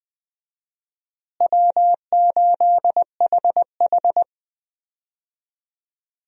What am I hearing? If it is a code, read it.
Morse code, heard as a keyed tone, about 20 wpm, 700 Hz. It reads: W8HH
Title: USA Call Signs - 4 Characters - Repeated 3x - 20wpm